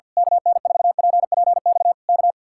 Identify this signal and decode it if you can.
Morse code, heard as a keyed tone, about 49 wpm, 700 Hz. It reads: KN4PPX K
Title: Speed-Racing Common QSO Phrases - 50wpm